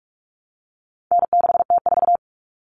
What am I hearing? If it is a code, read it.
Morse code, heard as a keyed tone, about 44 wpm, 700 Hz. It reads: N6T4
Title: All Cut Numbers - 4-Digit 45wpm